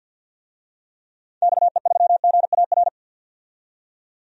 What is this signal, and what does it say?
Morse code, heard as a keyed tone, about 47 wpm, 700 Hz. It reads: XE3GAR
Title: DXCC-100 Call Signs - 6 Characters - 50wpm